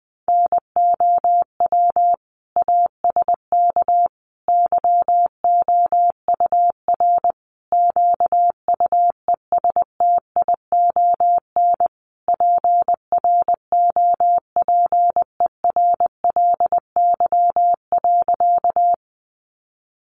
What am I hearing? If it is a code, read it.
Morse code, heard as a keyed tone, about 20 wpm, 700 Hz. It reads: NOW ASK YOUR QUESTION PROPERLY.
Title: Sentences from Top 1100 Words 20wpm